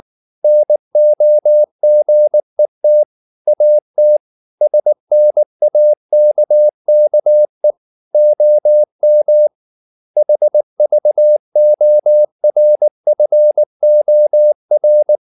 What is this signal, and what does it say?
Morse code, heard as a keyed tone, about 19 wpm, 600 Hz. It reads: NOGET AT SNAKKE OM HVORFOR
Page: Ferie 19 wpm